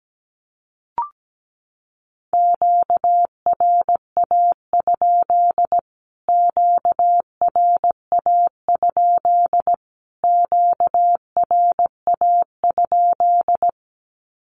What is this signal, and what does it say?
Morse code, heard as a keyed tone, about 17 wpm, 700 Hz. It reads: QRA? QRA? QRA?
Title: Q-Codes Repeated 3x 17wpm